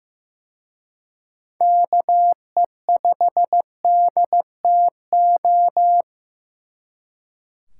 Morse code, heard as a keyed tone, about 18 wpm, 700 Hz. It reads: KE5DTO
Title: Speed Racing - USA 2x3 Call Signs 15wpm